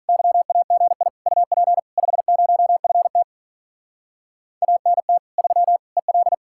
Morse code, heard as a keyed tone, about 44 wpm, 700 Hz. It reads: YAGI UP 50FT ANT 3 EL